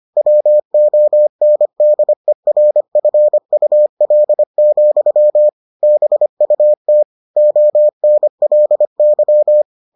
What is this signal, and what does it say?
Morse code, heard as a keyed tone, about 25 wpm, 600 Hz. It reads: WONDERFUL, BUT ONLY